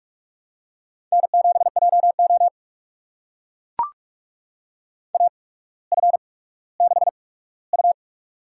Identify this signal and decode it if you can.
Morse code, heard as a keyed tone, about 44 wpm, spaced as for 19 wpm, 700 Hz. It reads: N7JO AF6U
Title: USA Call Signs - 4 Characters - 8x Character Spacing - 45wpm